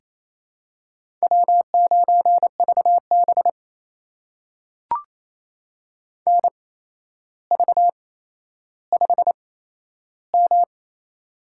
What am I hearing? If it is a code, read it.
Morse code, heard as a keyed tone, about 28 wpm, spaced as for 12 wpm, 700 Hz. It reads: W9VB NV5M
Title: USA Call Signs - 4 Characters - 8x Character Spacing - 28wpm